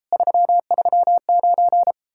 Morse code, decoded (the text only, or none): 339